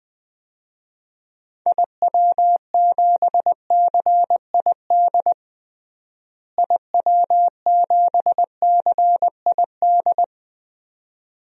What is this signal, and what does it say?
Morse code, heard as a keyed tone, about 20 wpm, 700 Hz. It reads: IW7CID IW7CID
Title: DXCC-100 Call Signs - 6 Characters - Repeated 2x - 20wpm